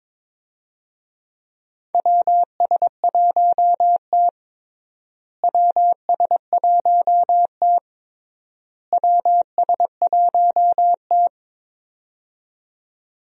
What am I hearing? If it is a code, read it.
Morse code, heard as a keyed tone, about 22 wpm, 700 Hz. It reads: WS1T WS1T WS1T
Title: USA Call Signs - 4 Characters - Repeated 3x - 22wpm